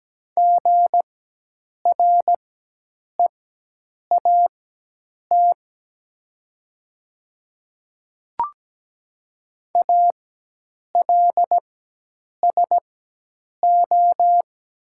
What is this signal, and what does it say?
Morse code, heard as a keyed tone, about 17 wpm, spaced as for 8 wpm, 700 Hz. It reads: GREAT ALSO
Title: 4x Character Spacing - Top 200 Words - 17wpm